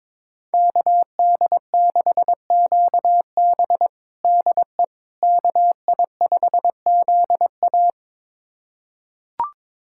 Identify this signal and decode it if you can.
Morse code, heard as a keyed tone, about 22 wpm, 700 Hz. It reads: KD6QB DE KI5ZA
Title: USA Callsign DE Callsign - 5 Characters - 22wpm